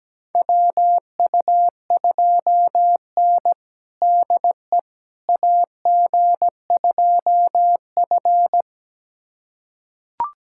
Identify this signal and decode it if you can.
Morse code, heard as a keyed tone, about 17 wpm, 700 Hz. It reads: WU2N DE AG2F
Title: USA Callsign DE Callsign - 4 Characters - 17wpm